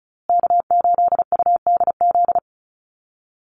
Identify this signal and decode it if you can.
Morse code, heard as a keyed tone, about 34 wpm, 700 Hz. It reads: K8UDZ
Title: USA Call Signs Repeated 3x 35wpm